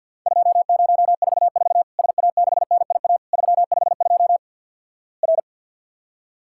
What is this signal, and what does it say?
Morse code, heard as a keyed tone, about 50 wpm, 700 Hz. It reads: 2044 SA6NIA 352 R
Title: Worked All Europe DX Contest - QTC Exchange with Repeats - 50wpm